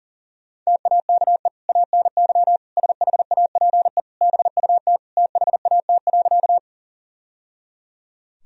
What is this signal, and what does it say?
Morse code, heard as a keyed tone, about 37 wpm, 700 Hz. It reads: TAKE ANY SHAPE BUT THAT.
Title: Sentences from Top 1000 Words - Review Part 3 - 40wpm